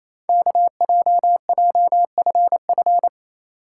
Morse code, heard as a keyed tone, about 28 wpm, 700 Hz. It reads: KJJFF
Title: Code Groups - 5 Letters - 28wpm